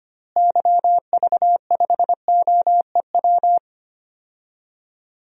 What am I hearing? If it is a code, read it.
Morse code, heard as a keyed tone, about 25 wpm, 700 Hz. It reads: YV5OEW
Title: DXCC-100 Call Signs - 6 Characters - Repeated 2x - 25wpm